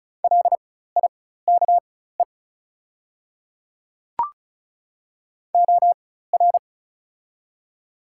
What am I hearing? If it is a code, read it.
Morse code, heard as a keyed tone, about 34 wpm, spaced as for 16 wpm, 700 Hz. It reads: LIKE OR
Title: 4x Character Spacing - Top 100 Words - 35wpm